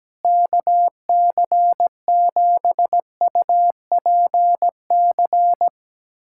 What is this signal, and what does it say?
Morse code, heard as a keyed tone, about 17 wpm, 700 Hz. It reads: KC7UPC
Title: USA Callsign DE Callsign 17wpm